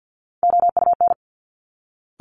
Morse code, heard as a keyed tone, about 50 wpm, 700 Hz. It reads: OUN
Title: Top Three Letter Combinations in Words 50wpm